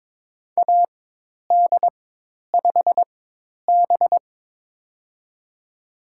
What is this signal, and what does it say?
Morse code, heard as a keyed tone, about 22 wpm, spaced as for 10 wpm, 700 Hz. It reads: AD5B
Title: USA Call Signs - 4 Characters - 4x Character Spacing - 22wpm